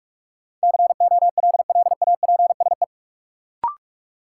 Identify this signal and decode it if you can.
Morse code, heard as a keyed tone, about 43 wpm, 700 Hz. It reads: COLLAPSE
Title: Sets of 5 Words Encore - New Word Drill - 45wpm